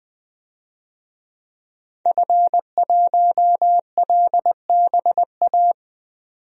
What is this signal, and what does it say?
Morse code, heard as a keyed tone, about 21 wpm, 700 Hz. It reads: F1LBA
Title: DXCC-100 Call Signs - 5 Characters - 20wpm